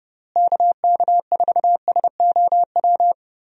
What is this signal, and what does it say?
Morse code, heard as a keyed tone, about 31 wpm, 700 Hz. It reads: KK4SOW
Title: USA Call Signs - 6 Characters - Repeated 2x - 30wpm